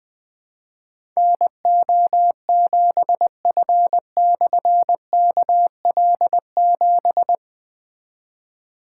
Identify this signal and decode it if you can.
Morse code, heard as a keyed tone, about 20 wpm, 700 Hz. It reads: NO7F/KL7
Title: Difficult Call Signs 20wpm